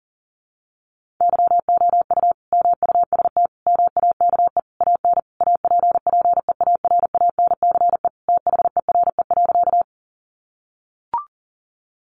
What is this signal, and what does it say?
Morse code, heard as a keyed tone, about 40 wpm, 700 Hz. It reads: YOU MUST MAKE AN APPEARANCE THERE.
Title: Sentences from Top 1100 Words 40wpm